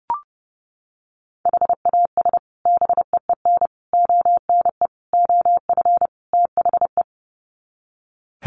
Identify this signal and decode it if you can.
Morse code, heard as a keyed tone, about 30 wpm, 700 Hz. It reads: HAS BEEN ONE OF THE
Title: Sets of 5 Words Encore 30wpm